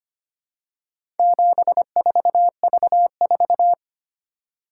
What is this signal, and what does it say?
Morse code, heard as a keyed tone, about 25 wpm, 700 Hz. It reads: 74V4
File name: practice / 25wpm / Four Letter-Number Combinations - 25wpm